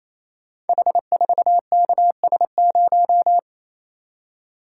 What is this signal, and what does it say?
Morse code, heard as a keyed tone, about 29 wpm, 700 Hz. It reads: H4KS0
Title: Code Groups - 5 Letter:Numbers - 28wpm